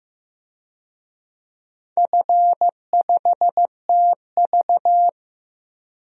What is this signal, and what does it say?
Morse code, heard as a keyed tone, about 15 wpm, 700 Hz. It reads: F5TV
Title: DXCC-100 Call Signs - 4 Characters - 15wpm